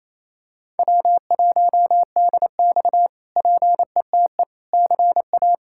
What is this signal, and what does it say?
Morse code, heard as a keyed tone, about 27 wpm, 700 Hz. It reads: W1DX PETE CA
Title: K1USN SST QSO Practice - 28wpm